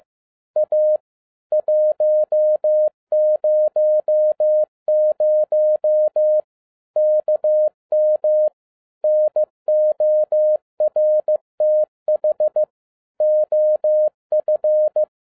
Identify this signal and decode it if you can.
Morse code, heard as a keyed tone, about 15 wpm, 610 Hz. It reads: A 100 KM NORTH OF
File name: 2011-07-mp3 / n47 JOHN K3TN 15wpm